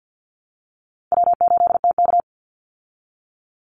Morse code, heard as a keyed tone, about 50 wpm, 700 Hz. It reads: W8TK
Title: Call Signs Repeated 3x - 50wpm